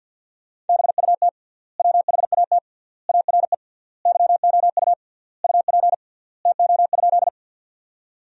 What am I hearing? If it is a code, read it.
Morse code, heard as a keyed tone, about 50 wpm, 700 Hz. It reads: BUT WHAT ARE YOU UP TO?
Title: 3x Word-Spacing - Sentences from Top 100 Words - 50wpm